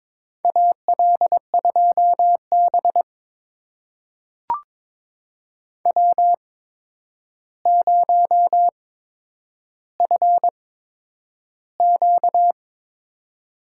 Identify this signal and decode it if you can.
Morse code, heard as a keyed tone, about 22 wpm, spaced as for 9 wpm, 700 Hz. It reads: AL2B W0FQ
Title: USA Call Signs - 8x Character Spacing - 22wpm